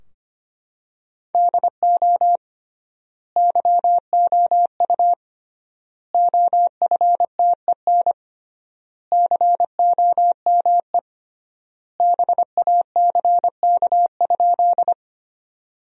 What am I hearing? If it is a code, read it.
Morse code, heard as a keyed tone, about 25 wpm, 700 Hz. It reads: DO YOU OFTEN COME BACK?
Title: 3x Word-Spacing - Sentences from Top 300 Words - 25wpm